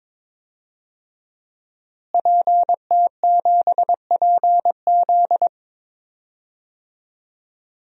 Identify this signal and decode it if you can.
Morse code, heard as a keyed tone, about 22 wpm, 700 Hz. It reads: PT7PZ
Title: DXCC-100 Call Signs - 5 Characters - Repeated 2x - 22wpm